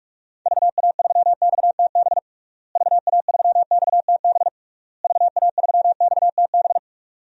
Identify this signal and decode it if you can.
Morse code, heard as a keyed tone, about 46 wpm, 700 Hz. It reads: VA3XTB VA3XTB VA3XTB
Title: DXCC-100 Call Signs - 6 Characters - Repeated 3x - 45wpm